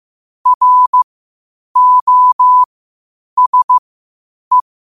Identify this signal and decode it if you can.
Morse code, heard as a keyed tone, about 15 wpm, spaced as for 8 wpm, 1000 Hz. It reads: ROSE
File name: qotdfiles / 8wpm